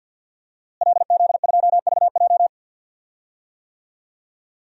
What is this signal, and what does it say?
Morse code, heard as a keyed tone, about 48 wpm, 700 Hz. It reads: LZ2VJ
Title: DXCC-100 Call Signs - 5 Characters - Repeated 2x - 50wpm